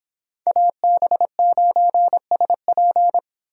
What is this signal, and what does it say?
Morse code, heard as a keyed tone, about 25 wpm, 700 Hz. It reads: AB9SP